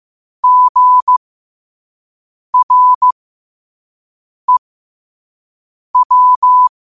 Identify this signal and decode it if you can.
Morse code, heard as a keyed tone, about 15 wpm, spaced as for 5 wpm, 1000 Hz. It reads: GREW
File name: qotdfiles / 5wpm